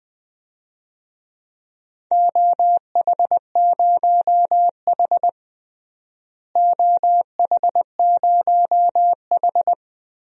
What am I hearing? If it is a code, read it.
Morse code, heard as a keyed tone, about 20 wpm, 700 Hz. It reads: OH0H OH0H
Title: DXCC-100 Call Signs - 4 Characters - Repeated 2x - 20wpm